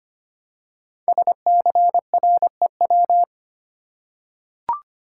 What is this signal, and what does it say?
Morse code, heard as a keyed tone, about 25 wpm, 700 Hz. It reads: SCREW
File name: 5 Letter Words - First 3 Letters Given - 25wpm